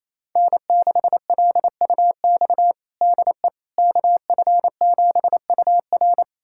Morse code, heard as a keyed tone, about 28 wpm, 700 Hz. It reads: N6LUX DE KF7UR
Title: USA Callsign DE Callsign - 5 Characters - 28wpm